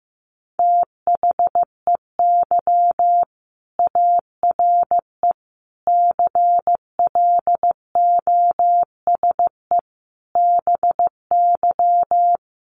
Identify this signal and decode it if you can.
Morse code, heard as a keyed tone, about 15 wpm, 700 Hz. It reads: THEY ARE CLOSE BY
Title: Sentences from Top 1400 Words 15wpm